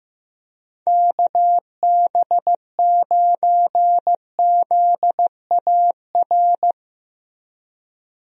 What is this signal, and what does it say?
Morse code, heard as a keyed tone, about 15 wpm, 700 Hz. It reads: KB9ZAR
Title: USA Call Signs - 6 Characters - Repeated 3x - 15wpm